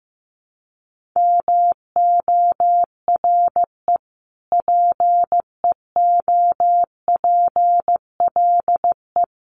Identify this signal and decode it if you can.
Morse code, heard as a keyed tone, about 15 wpm, 700 Hz. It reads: MORE PEOPLE
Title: Sentences from Top 1500 Words 15wpm